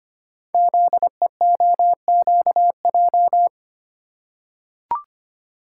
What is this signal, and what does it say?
Morse code, heard as a keyed tone, about 26 wpm, 700 Hz. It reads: ZEOQJ
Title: Code Groups - 5 Letters - 25wpm